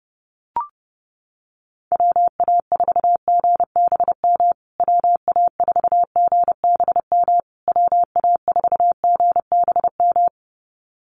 Morse code, heard as a keyed tone, about 30 wpm, 700 Hz. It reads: WA4GBM WA4GBM WA4GBM
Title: USA Call Signs Repeated 3x 30wpm